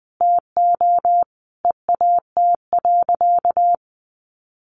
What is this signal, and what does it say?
Morse code, heard as a keyed tone, about 20 wpm, 700 Hz. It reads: TO EAT.
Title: Sentences from Top 1300 Words 20wpm